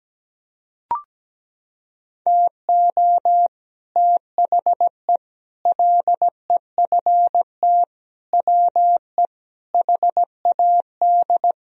Morse code, heard as a keyed tone, about 17 wpm, 700 Hz. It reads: TO THE LEFT WE HAD
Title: Sentences from Top 1000 Words - Review Part 1 - 17wpm